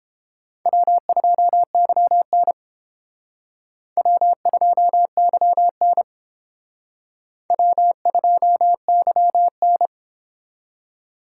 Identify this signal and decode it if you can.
Morse code, heard as a keyed tone, about 30 wpm, 700 Hz. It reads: W2YN W2YN W2YN